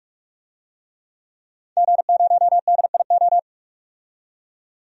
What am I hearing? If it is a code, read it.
Morse code, heard as a keyed tone, about 46 wpm, 700 Hz. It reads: G0DIO
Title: DXCC-100 Call Signs - 5 Characters - 45wpm